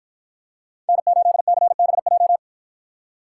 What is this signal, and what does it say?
Morse code, heard as a keyed tone, about 49 wpm, 700 Hz. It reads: N8CBJ